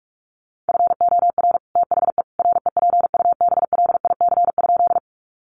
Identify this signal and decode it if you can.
Morse code, heard as a keyed tone, about 44 wpm, 700 Hz. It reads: FOR THE REPUBLIC?
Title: Sentences from Top 1500 Words 45wpm